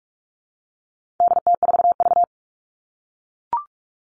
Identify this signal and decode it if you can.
Morse code, heard as a keyed tone, about 44 wpm, 700 Hz. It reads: DT4V
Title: All Cut Numbers - 4-Digit 45wpm